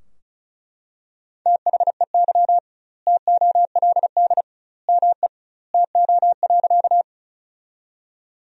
Morse code, heard as a keyed tone, about 36 wpm, 700 Hz. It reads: THEY TOLD ME TO.
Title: 2x Word-Spacing - Sentences from Top 400 Words - 35wpm